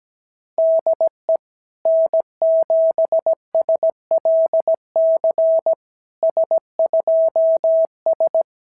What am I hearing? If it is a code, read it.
Morse code, heard as a keyed tone, about 17 wpm, 690 Hz. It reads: DE N7SLC S2S
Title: SOTA QSO Practice 17wpm